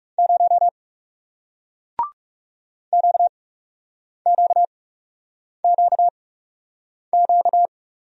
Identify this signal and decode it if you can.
Morse code, heard as a keyed tone, about 38 wpm, spaced as for 8 wpm, 700 Hz. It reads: 0 QQQQ